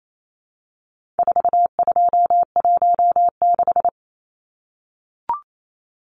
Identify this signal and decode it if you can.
Morse code, heard as a keyed tone, about 28 wpm, 700 Hz. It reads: 4216
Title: Numbers - 4-Digit 28wpm